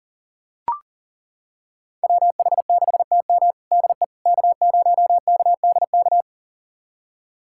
Morse code, heard as a keyed tone, about 40 wpm, 700 Hz. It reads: WH6TM DE K0KDK
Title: USA Callsign DE Callsign - 5 Characters - 40wpm